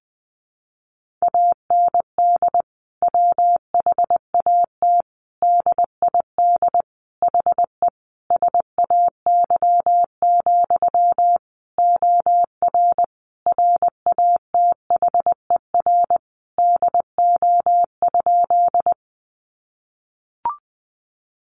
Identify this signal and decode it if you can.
Morse code, heard as a keyed tone, about 20 wpm, 700 Hz. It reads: AND WHAT DID HE SAY, OR RATHER DO?
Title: Sentences from Top 700 Words 20wpm